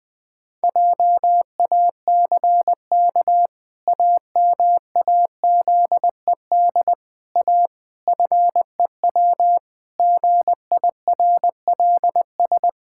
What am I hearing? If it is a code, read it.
Morse code, heard as a keyed tone, about 20 wpm, 700 Hz. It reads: JACK AMAZED A FEW GIRLS
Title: Pangram Challenge v2 - 20wpm